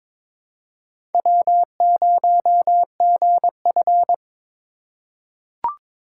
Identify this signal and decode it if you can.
Morse code, heard as a keyed tone, about 23 wpm, 700 Hz. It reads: W0GF
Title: USA Call Signs - 4 Characters - 4x Character Spacing - 22wpm